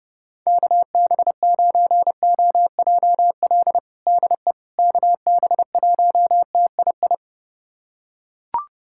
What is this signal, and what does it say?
Morse code, heard as a keyed tone, about 30 wpm, 700 Hz. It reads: KB9OJL DE KB1TII
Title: USA Callsign DE Callsign 30wpm